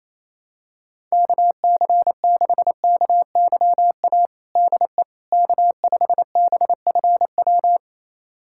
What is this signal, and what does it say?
Morse code, heard as a keyed tone, about 28 wpm, 700 Hz. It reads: KC6KYA DE K5BFW
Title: USA Callsign DE Callsign 28wpm